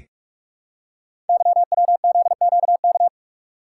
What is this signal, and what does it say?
Morse code, heard as a keyed tone, about 44 wpm, 690 Hz. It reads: YWZQK
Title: Code Groups - 5 Letters - 45wpm